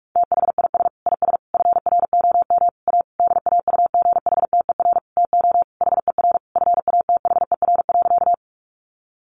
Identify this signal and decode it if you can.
Morse code, heard as a keyed tone, about 44 wpm, 700 Hz. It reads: THIS IS FROM A DAUGHTER TO HER FATHER.
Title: Sentences from Top 1500 Words 45wpm